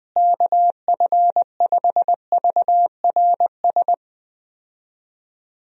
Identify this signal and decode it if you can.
Morse code, heard as a keyed tone, about 20 wpm, 700 Hz. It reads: KF5VRS
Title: USA Call Signs - 6 Characters - Repeated 2x - 20wpm